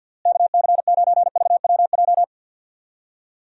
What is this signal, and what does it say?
Morse code, heard as a keyed tone, about 52 wpm, 700 Hz. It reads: KC9VWJ
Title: USA Call Signs - 6 Characters - Repeated 3x - 50wpm